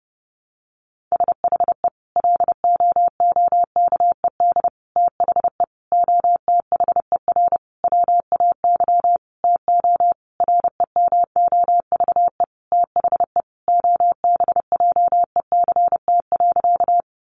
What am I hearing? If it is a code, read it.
Morse code, heard as a keyed tone, about 30 wpm, 700 Hz. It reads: SHE LOOKED THE OTHER WAY TO REMOVE THE OBJECT.
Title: Sentences from Top 800 Words 30wpm